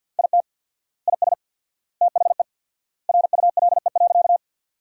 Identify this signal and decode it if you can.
Morse code, heard as a keyed tone, about 50 wpm, 700 Hz. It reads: IT IS THE RULE.
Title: 4x Word-Spacing - Sentences from Top 400 Words - 50wpm